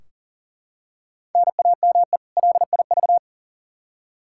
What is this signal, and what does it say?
Morse code, heard as a keyed tone, about 40 wpm, 700 Hz. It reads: NAME LIV
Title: Practice QSO - Exchange 1 - Beginning - 40wpm